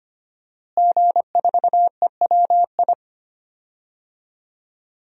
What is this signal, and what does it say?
Morse code, heard as a keyed tone, about 25 wpm, 700 Hz. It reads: G4EWI
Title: DXCC-100 Call Signs - 5 Characters - Repeated 2x - 25wpm